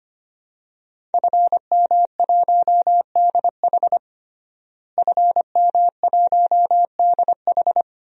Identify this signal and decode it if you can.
Morse code, heard as a keyed tone, about 25 wpm, 700 Hz. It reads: FM1DH FM1DH
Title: DXCC-100 Call Signs - 5 Characters - Repeated 2x - 25wpm